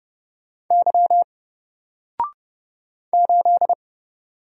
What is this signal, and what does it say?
Morse code, heard as a keyed tone, about 31 wpm, 700 Hz. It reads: Y 8
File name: Single-Letter-Number-30wpm